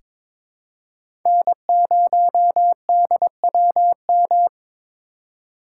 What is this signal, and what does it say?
Morse code, heard as a keyed tone, about 22 wpm, 700 Hz. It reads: N0DWM
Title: USA Call Signs - 5 Characters - 3x Character Spacing - 22wpm